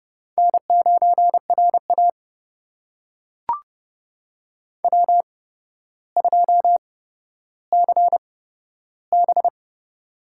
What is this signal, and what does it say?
Morse code, heard as a keyed tone, about 30 wpm, spaced as for 13 wpm, 700 Hz. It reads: N9RA W2CB
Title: USA Call Signs - 8x Character Spacing - 30wpm